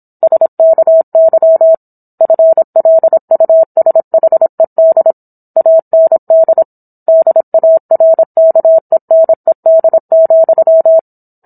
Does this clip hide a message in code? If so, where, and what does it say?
Yes: Morse code from 0.2 to 11.0 s, heard as a keyed tone, about 27 wpm, 650 Hz. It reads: SKY FLUSHED AND DARKENED,